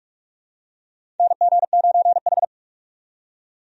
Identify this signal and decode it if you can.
Morse code, heard as a keyed tone, about 46 wpm, 700 Hz. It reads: NG9H
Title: USA Call Signs - 4 Characters - 45wpm